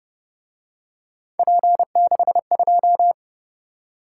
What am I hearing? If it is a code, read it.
Morse code, heard as a keyed tone, about 29 wpm, 700 Hz. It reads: P62